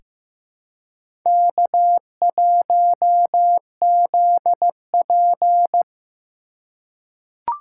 Morse code, heard as a keyed tone, about 15 wpm, 700 Hz. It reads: K1ZP
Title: USA Call Signs - 4 Characters - 4x Character Spacing - 15wpm